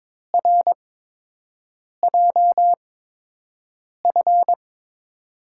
Morse code, heard as a keyed tone, about 22 wpm, spaced as for 6 wpm, 700 Hz. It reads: RJF
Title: USA Call Signs - 5 Characters - 8x Character Spacing - 22wpm